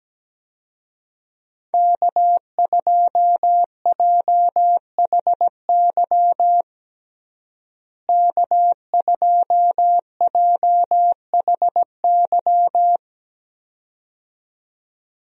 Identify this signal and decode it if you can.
Morse code, heard as a keyed tone, about 17 wpm, 700 Hz. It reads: K2JHY K2JHY
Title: USA Call Signs - 5 Characters - Repeated 2x - 17wpm